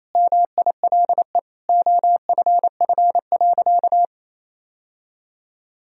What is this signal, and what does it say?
Morse code, heard as a keyed tone, about 28 wpm, 700 Hz. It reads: MILE OFF.
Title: Sentences from Top 1000 Words - Review Part 1 - 28wpm